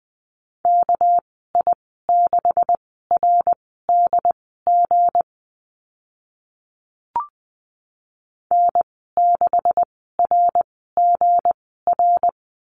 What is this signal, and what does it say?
Morse code, heard as a keyed tone, about 19 wpm, spaced as for 14 wpm, 700 Hz. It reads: KI6RDG N6RGR
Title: 200 Lesson 32g - Letter k - Call Signs 20wpm